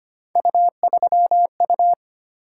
Morse code, heard as a keyed tone, about 25 wpm, 700 Hz. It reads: U3U
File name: Three Letter-Number Combinations - 25wpm